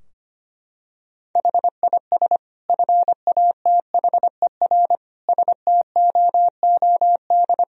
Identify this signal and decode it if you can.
Morse code, heard as a keyed tone, about 25 wpm, 700 Hz. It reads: HIS FATHER STOOD
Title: Sentences from Top 1000 Words - Review Part 2 - 25wpm